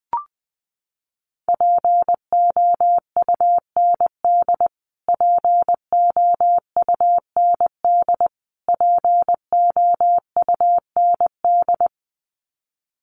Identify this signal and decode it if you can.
Morse code, heard as a keyed tone, about 20 wpm, 700 Hz. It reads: POUND POUND POUND
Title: Top 500 Words Repeated 20wpm